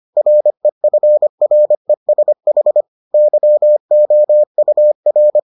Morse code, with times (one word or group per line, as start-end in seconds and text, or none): 0.2-2.8 REFRESH
3.1-5.4 YOUR